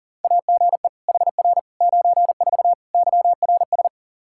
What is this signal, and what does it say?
Morse code, heard as a keyed tone, about 40 wpm, 700 Hz. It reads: AGE HR 94 YRS
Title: Practice QSO - Exchange 3 - More Info - 40wpm